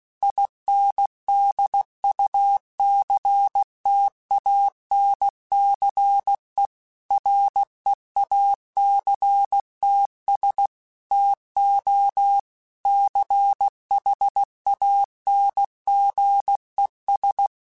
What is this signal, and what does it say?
Morse code, heard as a keyed tone, about 16 wpm, 800 Hz. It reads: INDUCTANCE REACTS TO CHANGES
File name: Technical Subjects / Technical Subjects-mp3s / Reactance @15